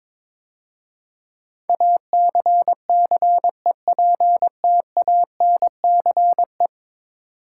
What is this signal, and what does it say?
Morse code, heard as a keyed tone, about 22 wpm, 700 Hz. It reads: ACCEPTANCE
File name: practice / 22wpm / 10 Letter Words - First 3 Letters Given - 22wpm